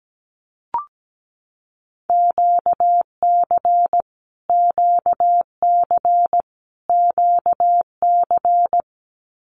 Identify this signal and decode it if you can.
Morse code, heard as a keyed tone, about 17 wpm, 700 Hz. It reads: QC QC QC
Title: Canadian Provinces and Territories - Repeated 3x - 17wpm